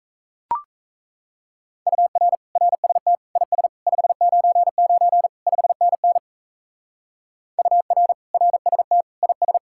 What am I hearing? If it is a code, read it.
Morse code, heard as a keyed tone, about 35 wpm, 700 Hz. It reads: UR RST IS 599 5NN UR RST IS
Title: Speed-Racing Common QSO Phrases Encore - 28wpm